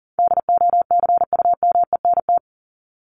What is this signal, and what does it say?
Morse code, heard as a keyed tone, about 38 wpm, 700 Hz. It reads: DOCUMENT
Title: Top 1751-2000 Words 40wpm